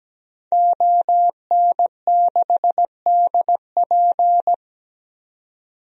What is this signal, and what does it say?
Morse code, heard as a keyed tone, about 17 wpm, 700 Hz. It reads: ON6DP
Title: DXCC-100 Call Signs - 5 Characters - 17wpm